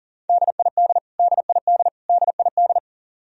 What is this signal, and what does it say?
Morse code, heard as a keyed tone, about 40 wpm, 700 Hz. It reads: DID DID DID
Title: Top 100 Words Repeated 40wpm